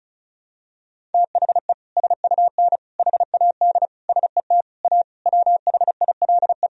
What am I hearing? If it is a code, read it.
Morse code, heard as a keyed tone, about 36 wpm, 700 Hz. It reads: THE SUN HAD SET A WHILE
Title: Sentences from Top 1000 Words - Review Part 1 - 35wpm